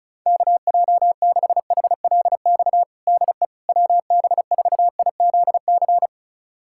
Morse code, heard as a keyed tone, about 36 wpm, 700 Hz. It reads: KJ6HLX DE WB4IZC
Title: USA Callsign DE Callsign - 6 Characters - 35wpm